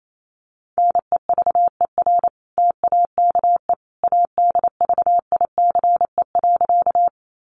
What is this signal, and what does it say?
Morse code, heard as a keyed tone, about 28 wpm, 700 Hz. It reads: NEVER TAKE ADVICE.
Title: Sentences from Top 1100 Words 28wpm